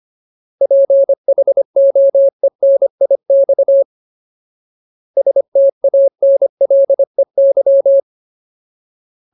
Morse code, heard as a keyed tone, about 25 wpm, 550 Hz. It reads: PHOENIX STANLEY
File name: Ultimate Word List / 25WPM Ultimate